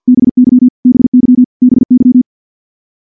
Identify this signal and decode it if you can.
Morse code, heard as a keyed tone, about 46 wpm, 250 Hz. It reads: BY BY BY
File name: morse / morse-training-50WPM